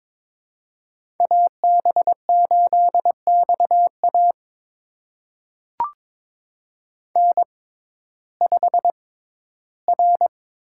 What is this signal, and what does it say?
Morse code, heard as a keyed tone, about 22 wpm, spaced as for 13 wpm, 700 Hz. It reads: AB8XA N5R
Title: USA Call Signs - 5 Characters - 6x Character Spacing - 22wpm